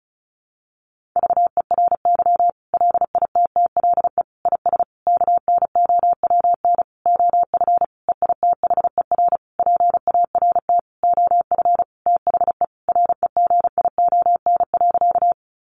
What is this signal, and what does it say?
Morse code, heard as a keyed tone, about 34 wpm, 700 Hz. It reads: VERY LITTLE IS KNOWN OF EITHER PART OF THE REGION.
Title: Sentences from Top 1500 Words 35wpm